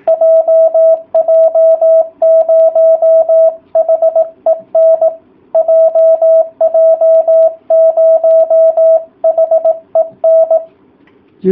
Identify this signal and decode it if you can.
Morse code, heard as a keyed tone, about 18 wpm, 660 Hz. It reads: JJ0HEN JJ0HEN